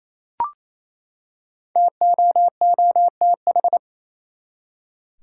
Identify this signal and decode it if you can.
Morse code, heard as a keyed tone, about 28 wpm, 700 Hz. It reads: TOOTH
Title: Five Letter Words - 28wpm